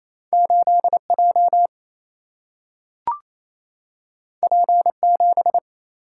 Morse code, heard as a keyed tone, about 28 wpm, 700 Hz. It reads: 8J P7
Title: Single-Letter Single-Number 28wpm